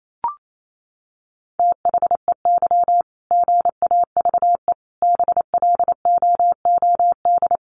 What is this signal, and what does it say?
Morse code, heard as a keyed tone, about 28 wpm, 700 Hz. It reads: THEY GAVE BLOOD
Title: Sentences from Top 1400 Words 28wpm